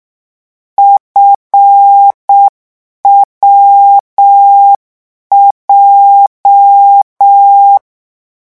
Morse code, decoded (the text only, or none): FWJ